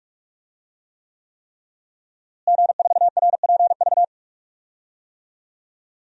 Morse code, heard as a keyed tone, about 43 wpm, 700 Hz. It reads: G4RPV
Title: DXCC-100 Call Signs - 5 Characters - Repeated 3x - 45wpm